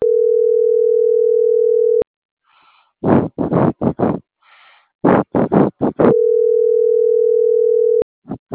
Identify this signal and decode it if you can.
Ringback tone (North America), 440 + 480 Hz.